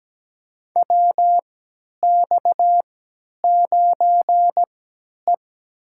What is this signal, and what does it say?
Morse code, heard as a keyed tone, about 17 wpm, spaced as for 10 wpm, 700 Hz. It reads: WX9E
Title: USA Call Signs - 4 Characters - 3x Character Spacing - 17wpm